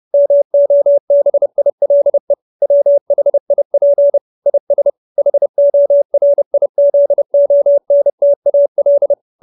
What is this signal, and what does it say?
Morse code, heard as a keyed tone, about 30 wpm, 570 Hz. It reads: MOBILE WHIP IS HORIZONTAL
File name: QSO317 30